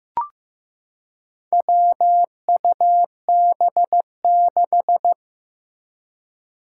Morse code, heard as a keyed tone, about 15 wpm, 700 Hz. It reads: WUB6
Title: Four Challenging Letter-Number Combinations 15wpm